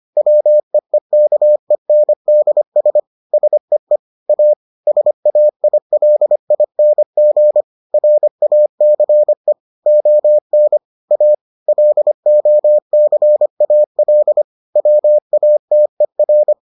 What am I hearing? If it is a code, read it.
Morse code, heard as a keyed tone, about 25 wpm, 600 Hz. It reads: WEEKENDS SEE A SAILING RACE ON A LOCAL WATER